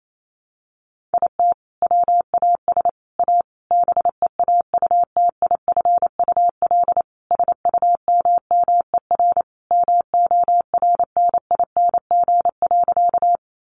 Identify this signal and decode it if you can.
Morse code, heard as a keyed tone, about 28 wpm, 700 Hz. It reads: IT WAS A BEAUTIFUL SUMMER MORNING.
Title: Sentences from Top 1100 Words 28wpm